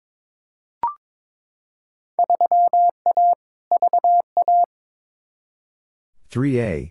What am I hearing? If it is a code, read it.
Morse code, heard as a keyed tone, about 22 wpm, 700 Hz. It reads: 3A VA
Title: Field Day Exchanges - 22wpm